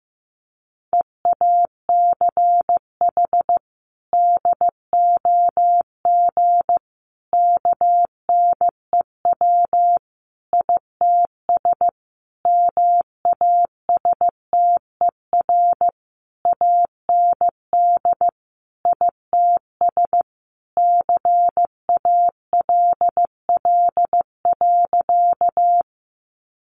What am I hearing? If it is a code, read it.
Morse code, heard as a keyed tone, about 15 wpm, 700 Hz. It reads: EACH DOG KNEW ITS MASTER AND ITS CALL.
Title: Sentences from Top 1300 Words 15wpm